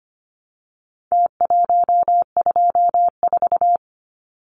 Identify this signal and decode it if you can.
Morse code, heard as a keyed tone, about 25 wpm, 700 Hz. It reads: T124
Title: Numbers - Contest Style - 4-Digit 25wpm